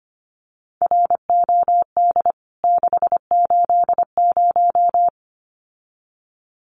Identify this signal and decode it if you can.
Morse code, heard as a keyed tone, about 25 wpm, 700 Hz. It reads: ROD 680
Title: CWT Exchange 25wpm